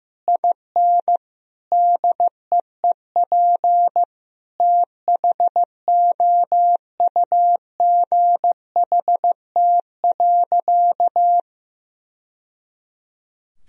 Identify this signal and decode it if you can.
Morse code, heard as a keyed tone, about 15 wpm, 700 Hz. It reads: IN DEEP THOUGHT.
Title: Sentences from Top 1000 Words - Review Part 3 - 15wpm